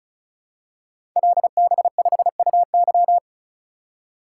Morse code, heard as a keyed tone, about 36 wpm, 700 Hz. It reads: LB5UY
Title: Code Groups - 5 Letter:Numbers - 35wpm